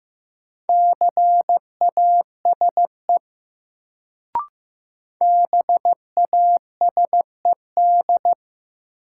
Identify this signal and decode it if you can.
Morse code, heard as a keyed tone, about 15 wpm, 700 Hz. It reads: CASE BASED
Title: Sets of 2 Words - New Word Drill - 15wpm